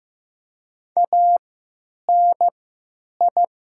Morse code, heard as a keyed tone, about 15 wpm, 700 Hz. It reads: A N I
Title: 3x Character Spacing - Top 200 Words - 15wpm